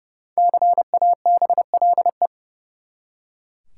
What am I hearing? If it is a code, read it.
Morse code, heard as a keyed tone, about 32 wpm, 700 Hz. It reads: CABLE
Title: Five Letter Words - 30wpm